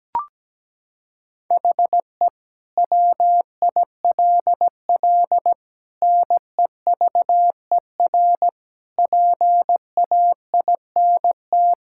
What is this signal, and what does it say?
Morse code, heard as a keyed tone, about 17 wpm, 700 Hz. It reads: HE WILL NEVER PAINT
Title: Sentences from Top 1000 Words - Review Part 6 - 17wpm